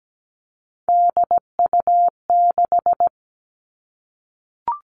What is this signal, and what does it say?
Morse code, heard as a keyed tone, about 17 wpm, 700 Hz. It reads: DU6
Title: Three Challenging Letter-Number Combinations 17wpm